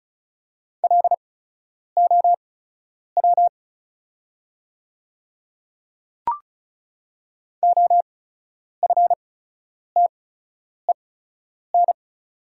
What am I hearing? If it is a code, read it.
Morse code, heard as a keyed tone, about 34 wpm, spaced as for 10 wpm, 690 Hz. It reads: LOW OFTEN
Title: 8x Character Spacing - Top 500 Words - 35wpm